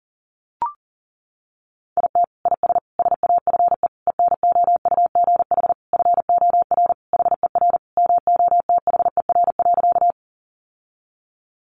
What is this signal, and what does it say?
Morse code, heard as a keyed tone, about 40 wpm, 700 Hz. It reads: IT IS SAFE ENOUGH FOR HER MOTHER.
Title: Sentences from Top 900 Words 40wpm